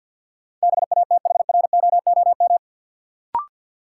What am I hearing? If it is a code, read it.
Morse code, heard as a keyed tone, about 52 wpm, 700 Hz. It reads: BATHROOM
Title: Sets of 5 Words Encore - New Word Drill - 50wpm